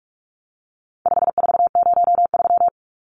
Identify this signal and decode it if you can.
Morse code, heard as a keyed tone, about 44 wpm, 700 Hz. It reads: 5403
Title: Numbers - 4-Digit 45wpm